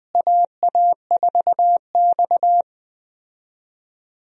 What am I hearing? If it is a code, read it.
Morse code, heard as a keyed tone, about 21 wpm, 690 Hz. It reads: AA4X
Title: USA Call Signs - 4 Characters - Repeated 3x - 20wpm